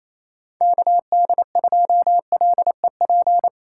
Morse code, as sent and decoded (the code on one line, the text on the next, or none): -.- -.. ..--- .-.. .--.
KD2LEP